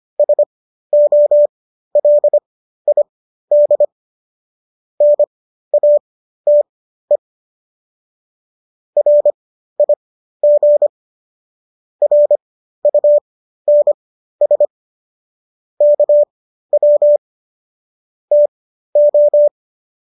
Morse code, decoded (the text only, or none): SOLID NATE RIG RUNS KW TO